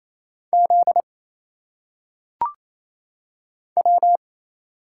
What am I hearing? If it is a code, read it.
Morse code, heard as a keyed tone, about 28 wpm, 700 Hz. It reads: Z W
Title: Single Letters - Phonetic Pronunciation 28wpm